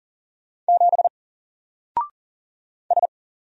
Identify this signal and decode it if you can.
Morse code, heard as a keyed tone, about 40 wpm, 700 Hz. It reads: 7 S
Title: Single Letter-Number Alt 40wpm